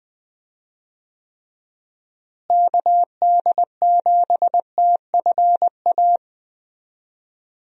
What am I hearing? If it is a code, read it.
Morse code, heard as a keyed tone, about 21 wpm, 700 Hz. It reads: KD7TFA
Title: USA Call Signs - 6 Characters - Repeated 3x - 20wpm